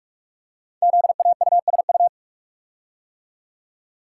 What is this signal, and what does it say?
Morse code, heard as a keyed tone, about 44 wpm, 690 Hz. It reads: ZAUSU
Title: Code Groups - 5 Letters - 45wpm